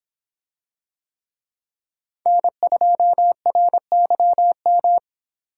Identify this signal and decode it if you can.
Morse code, heard as a keyed tone, about 24 wpm, 700 Hz. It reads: N2RYM